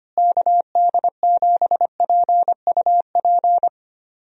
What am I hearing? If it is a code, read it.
Morse code, heard as a keyed tone, about 25 wpm, 700 Hz. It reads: KD7PUP
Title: USA Call Signs - 6 Characters - Repeated 2x - 25wpm